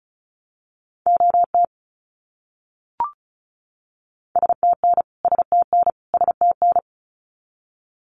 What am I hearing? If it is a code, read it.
Morse code, heard as a keyed tone, about 34 wpm, 700 Hz. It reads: OT STN STN STN
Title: Abbreviations Repeated 3x - 35wpm